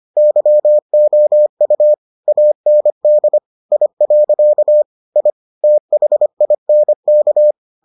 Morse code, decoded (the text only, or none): YOU AND I. I THINK